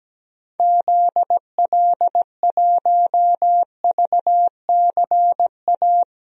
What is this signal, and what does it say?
Morse code, heard as a keyed tone, about 17 wpm, 700 Hz. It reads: ZL1VCA
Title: DXCC-100 Call Signs - 6 Characters - 17wpm